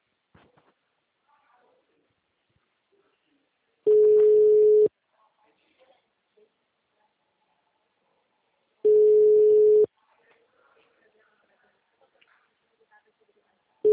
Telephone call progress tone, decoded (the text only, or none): ringback tone